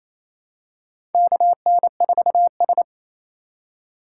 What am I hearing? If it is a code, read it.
Morse code, heard as a keyed tone, about 27 wpm, 700 Hz. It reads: KN4S
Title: USA Call Signs - 4 Characters - 4x Character Spacing - 28wpm